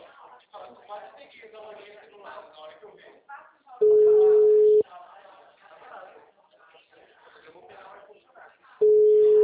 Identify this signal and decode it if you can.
Ringback tone (Europe, 425 Hz), with speech.